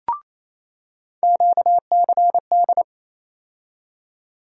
Morse code, heard as a keyed tone, about 28 wpm, 700 Hz. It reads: QCD
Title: USA Call Sign Postfixes - 3 Letters 28wpm